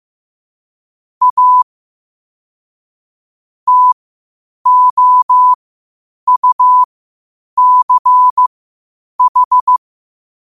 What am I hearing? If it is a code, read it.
Morse code, heard as a keyed tone, about 15 wpm, spaced as for 8 wpm, 1000 Hz. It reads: A TOUCH
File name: qotdfiles / 8wpm